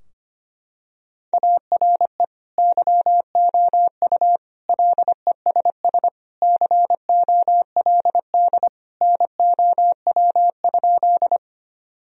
Morse code, heard as a keyed tone, about 25 wpm, 700 Hz. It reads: ARE YOU LESS COLD NOW?
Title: Sentences from Top 1000 Words - Review Part 2 - 25wpm